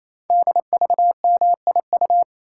Morse code, heard as a keyed tone, about 27 wpm, 700 Hz. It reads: DVMIU
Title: Code Groups - 5 Letters - 28wpm